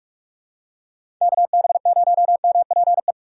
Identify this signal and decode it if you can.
Morse code, heard as a keyed tone, about 46 wpm, 700 Hz. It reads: KB0MPE
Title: USA Call Signs - 3x Character Spacing - 45wpm